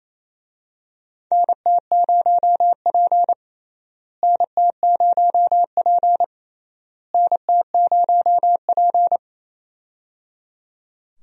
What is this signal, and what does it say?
Morse code, heard as a keyed tone, about 28 wpm, 700 Hz. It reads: NT0P NT0P NT0P
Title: USA Call Signs - 4 Characters - Repeated 3x - 28wpm